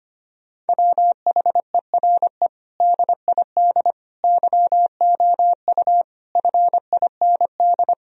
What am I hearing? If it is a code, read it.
Morse code, heard as a keyed tone, about 25 wpm, 700 Hz. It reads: WHERE DID YOU FIND
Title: Sentences from Top 1000 Words - Review Part 2 - 25wpm